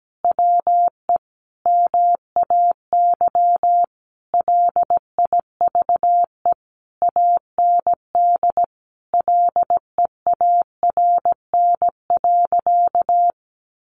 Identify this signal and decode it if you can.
Morse code, heard as a keyed tone, about 17 wpm, 700 Hz. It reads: WE MAY LIVE AND LEARN.
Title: Sentences from Top 800 Words 17wpm